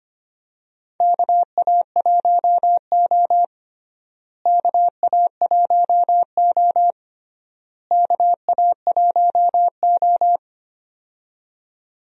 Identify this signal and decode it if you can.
Morse code, heard as a keyed tone, about 25 wpm, 700 Hz. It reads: KA1O KA1O KA1O
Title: USA Call Signs - 4 Characters - Repeated 3x - 25wpm